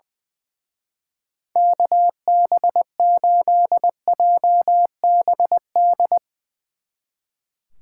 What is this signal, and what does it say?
Morse code, heard as a keyed tone, about 20 wpm, 700 Hz. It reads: KB8JBD
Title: Speed Racing - USA 2x3 Call Signs 20wpm